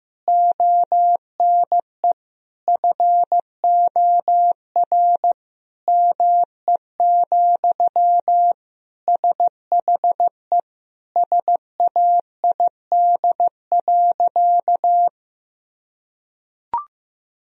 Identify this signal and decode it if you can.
Morse code, heard as a keyed tone, about 16 wpm, 700 Hz. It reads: ONE FOR ME, SHE SAID.
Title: Sentences from Top 1000 Words - Review Part 1 - 15wpm